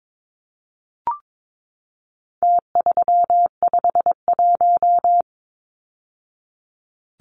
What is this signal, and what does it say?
Morse code, heard as a keyed tone, about 22 wpm, 700 Hz. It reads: T351
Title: Numbers - Contest Style - 4-Digit 22wpm